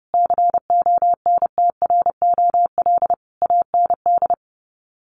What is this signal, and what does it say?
Morse code, heard as a keyed tone, about 30 wpm, 700 Hz. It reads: CONTROL AND
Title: Sets of 5 Words Encore2 30wpm